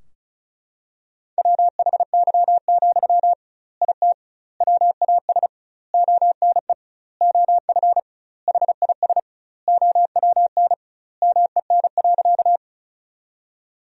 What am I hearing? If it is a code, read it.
Morse code, heard as a keyed tone, about 34 wpm, 700 Hz. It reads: WHY, IT WAS ONE OF HIS OWN MEN.
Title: 2x Word-Spacing - Sentences from Top 300 Words - 35wpm